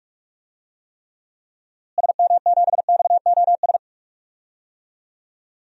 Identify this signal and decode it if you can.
Morse code, heard as a keyed tone, about 46 wpm, 700 Hz. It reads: SM7XOS
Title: DXCC-100 Call Signs - 6 Characters - Repeated 2x - 45wpm